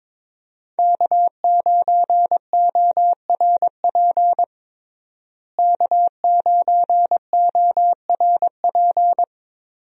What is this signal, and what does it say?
Morse code, heard as a keyed tone, about 21 wpm, 700 Hz. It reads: K9ORP K9ORP
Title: USA Call Signs - 5 Characters - Repeated 2x - 22wpm